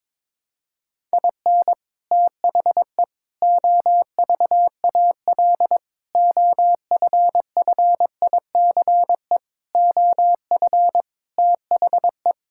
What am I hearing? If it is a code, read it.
Morse code, heard as a keyed tone, about 22 wpm, 700 Hz. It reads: IN THE OVAL OFFICE OF THE
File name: practice / 22wpm / Sets of 8 Words - 22wpm